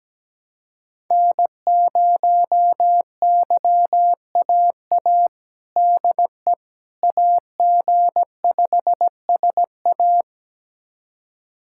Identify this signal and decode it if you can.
Morse code, heard as a keyed tone, about 17 wpm, 700 Hz. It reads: N0YAA DE AG5SA
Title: USA Callsign DE Callsign 17wpm